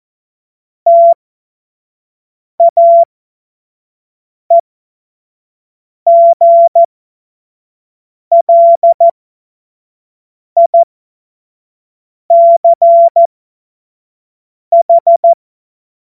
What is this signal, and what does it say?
Morse code, heard as a keyed tone, about 14 wpm, spaced as for 5 wpm, 680 Hz. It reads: TAEGLICH
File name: Morsekurs / Witze / Witz 03. August 2021, 25 F70